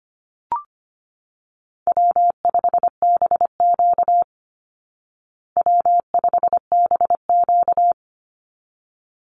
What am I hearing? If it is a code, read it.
Morse code, heard as a keyed tone, about 25 wpm, 700 Hz. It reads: W5BQ W5BQ
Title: Easy Call Signs Repeated 2x - 25wpm